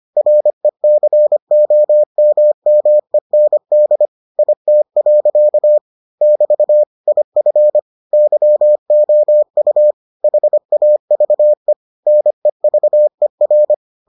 Morse code, heard as a keyed tone, about 25 wpm, 600 Hz. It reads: RECOMMEND IT. <BT> IF YOU HAVE NEVER